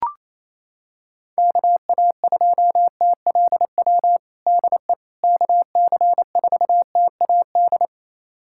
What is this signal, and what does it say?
Morse code, heard as a keyed tone, about 28 wpm, 700 Hz. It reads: KA2TLW DE KC4TAD
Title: USA Callsign DE Callsign - 6 Characters - 28wpm